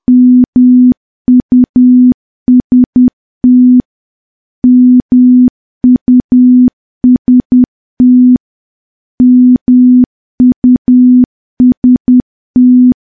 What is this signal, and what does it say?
Morse code, heard as a keyed tone, about 10 wpm, 250 Hz. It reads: MUST MUST MUST
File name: morse / morse-training-10WPM